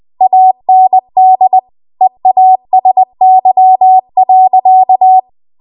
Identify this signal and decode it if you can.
Morse code, heard as a keyed tone, about 20 wpm, 750 Hz. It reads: AND EASY.